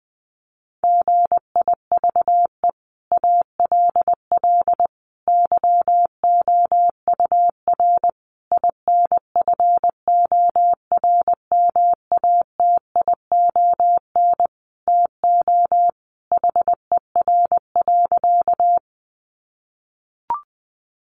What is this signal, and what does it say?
Morse code, heard as a keyed tone, about 20 wpm, 700 Hz. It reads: GIVE ALL YOUR INFORMATION TO HER.
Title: Sentences from Top 600 Words 20wpm